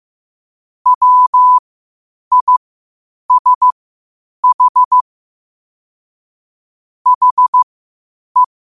Morse code, heard as a keyed tone, about 15 wpm, spaced as for 8 wpm, 1000 Hz. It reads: WISH HE
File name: qotdfiles / 8wpm